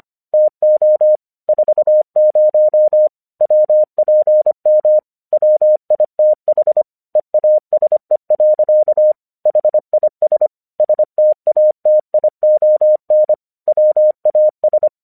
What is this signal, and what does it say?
Morse code, heard as a keyed tone, about 25 wpm, 610 Hz. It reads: TO 40 WPM WITH EASE. HIS STATION WAS
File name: 1 English 25wpm / AQC11